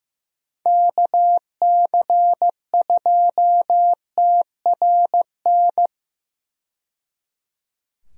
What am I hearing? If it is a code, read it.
Morse code, heard as a keyed tone, about 15 wpm, 700 Hz. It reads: KC2TRN
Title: USA Call Signs - 6 Characters - 15wpm